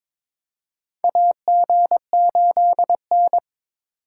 Morse code, heard as a keyed tone, about 22 wpm, 700 Hz. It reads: AG8N
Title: USA Call Signs - 4 Characters - 2x Character Spacing - 22wpm